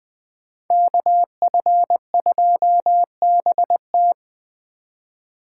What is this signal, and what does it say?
Morse code, heard as a keyed tone, about 20 wpm, 700 Hz. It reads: KF2BT
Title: Speed Racing - USA 2x2 Call Signs 17wpm